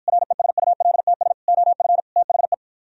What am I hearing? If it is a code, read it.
Morse code, heard as a keyed tone, about 52 wpm, 700 Hz. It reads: RESULTS OF THE